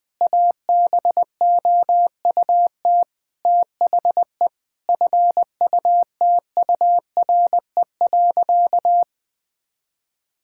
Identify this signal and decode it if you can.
Morse code, heard as a keyed tone, about 21 wpm, 700 Hz. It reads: ABOUT THE FUTURE.
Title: Sentences from Top 1000 Words - Review Part 5 - 20wpm